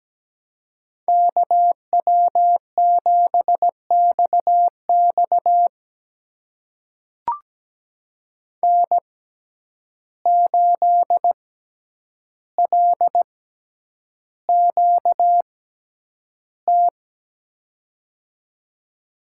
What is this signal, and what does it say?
Morse code, heard as a keyed tone, about 17 wpm, spaced as for 9 wpm, 700 Hz. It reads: KW7XX N8LQT
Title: USA Call Signs - 5 Characters - 6x Character Spacing - 17wpm